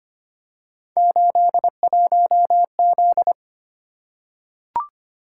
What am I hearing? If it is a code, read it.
Morse code, heard as a keyed tone, about 25 wpm, 700 Hz. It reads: 81Z
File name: Three Letter-Number Combinations - 25wpm